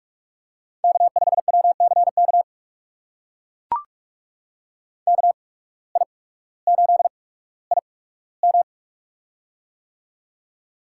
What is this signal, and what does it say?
Morse code, heard as a keyed tone, about 44 wpm, spaced as for 19 wpm, 700 Hz. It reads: K5WCK KI8IM
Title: USA Call Signs - 5 Characters - 8x Character Spacing - 45wpm